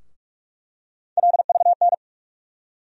Morse code, heard as a keyed tone, about 46 wpm, 700 Hz. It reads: LVN
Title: USA Call Sign Postfixes - 3 Letters 45wpm